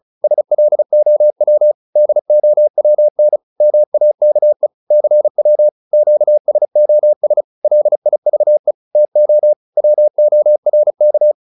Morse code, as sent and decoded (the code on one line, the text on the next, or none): ... .-.. --- .-- / -.. --- .-- -. / -- .- -.- . / -.-. .-- / --.- ... --- ... / .-.. .. ...- . / - --- / .-- --- .-. -.-
SLOW DOWN MAKE CW QSOS LIVE TO WORK